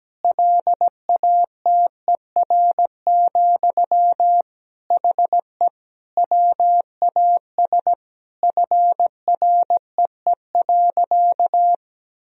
Morse code, heard as a keyed tone, about 17 wpm, 700 Hz. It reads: LATER, HE WAS FREE.
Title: Sentences from Top 1000 Words - Review Part 4 - 17wpm